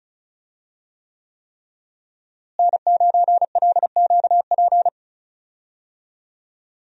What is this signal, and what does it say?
Morse code, heard as a keyed tone, about 34 wpm, 700 Hz. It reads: N9LQP
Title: USA Call Signs - 5 Characters - Repeated 3x - 35wpm